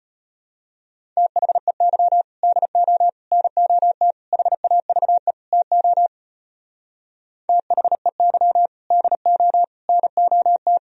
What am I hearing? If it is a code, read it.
Morse code, heard as a keyed tone, about 32 wpm, 700 Hz. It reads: THEY DO NOT HAVE TO THEY DO NOT